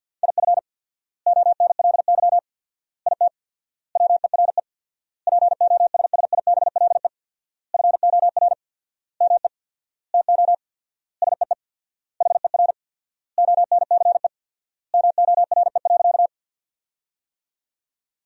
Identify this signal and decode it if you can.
Morse code, heard as a keyed tone, about 52 wpm, 700 Hz. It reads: IF ONLY IT WERE POSSIBLE FOR ME TO SEE HER ONCE MORE.
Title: 4x Word-Spacing - Sentences from Top 500 Words - 50wpm